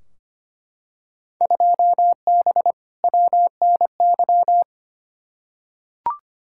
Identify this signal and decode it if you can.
Morse code, heard as a keyed tone, about 25 wpm, 700 Hz. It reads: 2B WNY
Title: Field Day Exchanges Encore - 25wpm